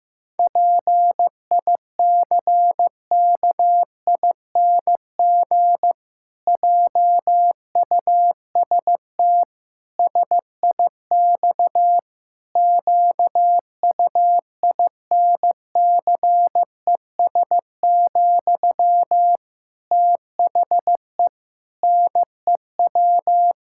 Morse code, heard as a keyed tone, about 15 wpm, 700 Hz. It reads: PICKING JUST SIX QUINCES, THE NEW
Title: Pangram Challenge v2 - 15wpm